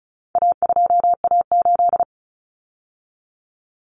Morse code, heard as a keyed tone, about 34 wpm, 700 Hz. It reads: A2A8
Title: Common Cut Numbers - 4-Digit 35wpm